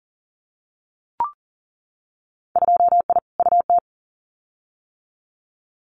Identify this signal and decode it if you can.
Morse code, heard as a keyed tone, about 41 wpm, 700 Hz. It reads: 2I UT
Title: Winter Field Day Exchanges 40wpm